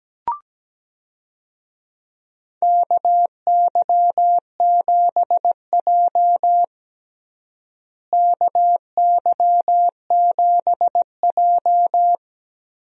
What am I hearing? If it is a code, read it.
Morse code, heard as a keyed tone, about 17 wpm, 700 Hz. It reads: KY7J KY7J
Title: USA Call Signs - 4 Characters - Repeated 3x - 17wpm